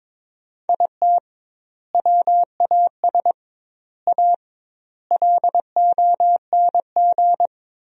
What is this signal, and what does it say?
Morse code, heard as a keyed tone, about 22 wpm, 700 Hz. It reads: IT WAS A LONG
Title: 2x Word-Spacing - Sentences from Top 500 Words - 22wpm